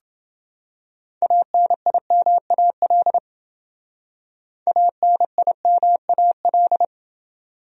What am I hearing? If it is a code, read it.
Morse code, heard as a keyed tone, about 29 wpm, 700 Hz. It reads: ANIMAL ANIMAL